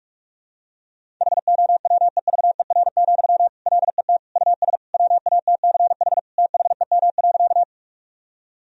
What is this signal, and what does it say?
Morse code, heard as a keyed tone, about 46 wpm, 700 Hz. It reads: HOWEVER, LET US WATCH THEM.
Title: Sentences from Top 1000 Words - Review Part 1 - 45wpm